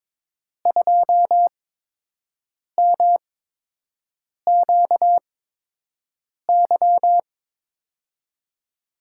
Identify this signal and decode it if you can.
Morse code, heard as a keyed tone, about 22 wpm, spaced as for 6 wpm, 700 Hz. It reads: 2MQY
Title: USA Call Signs - 5 Characters - 8x Character Spacing - 22wpm